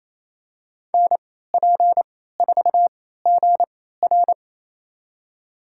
Morse code, heard as a keyed tone, about 28 wpm, spaced as for 16 wpm, 700 Hz. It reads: NP4GR
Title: USA Call Signs - 3x Character Spacing - 28wpm